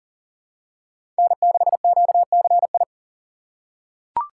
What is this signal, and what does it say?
Morse code, heard as a keyed tone, about 40 wpm, 700 Hz. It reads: N6QCI
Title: USA Call Signs - 5 Characters - 3x Character Spacing - 40wpm